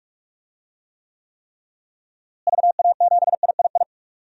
Morse code, heard as a keyed tone, about 46 wpm, 700 Hz. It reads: VA7III
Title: DXCC-100 Call Signs - 6 Characters - Repeated 3x - 45wpm